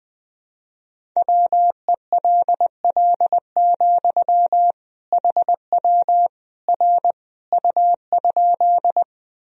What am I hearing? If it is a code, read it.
Morse code, heard as a keyed tone, about 20 wpm, 700 Hz. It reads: WELL, HW R U?
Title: Sentences from Top 200 Words and Abbreviations 20wpm